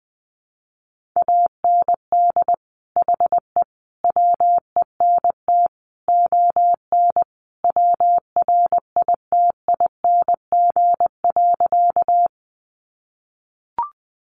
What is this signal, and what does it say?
Morse code, heard as a keyed tone, about 20 wpm, 700 Hz. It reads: AND HE WENT ON WRITING.
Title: Sentences from Top 800 Words 20wpm